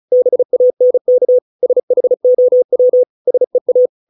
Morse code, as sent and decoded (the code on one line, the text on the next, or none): -... .- -. -.- / ... .... --- .-- / ... .-
BANK SHOW SEA